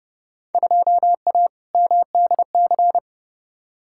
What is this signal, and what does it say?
Morse code, heard as a keyed tone, about 31 wpm, 700 Hz. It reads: 2A MDC
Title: Field Day Exchanges - 30wpm